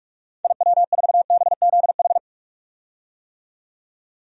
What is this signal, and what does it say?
Morse code, heard as a keyed tone, about 46 wpm, 700 Hz. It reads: IW4BZH
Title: DXCC-100 Call Signs - 6 Characters - 45wpm